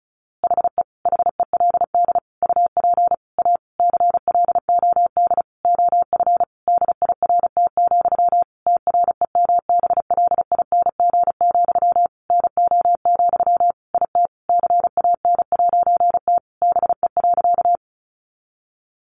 Morse code, heard as a keyed tone, about 34 wpm, 700 Hz. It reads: HE HELD UP A CLOD OF DIRT, TREMBLING, NO, IT CAN'T BE.
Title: A Sound Of Thunder 35wpm